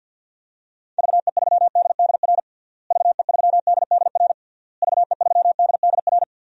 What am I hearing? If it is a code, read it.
Morse code, heard as a keyed tone, about 51 wpm, 700 Hz. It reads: VE3DDR VE3DDR VE3DDR
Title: DXCC-100 Call Signs - 6 Characters - Repeated 3x - 50wpm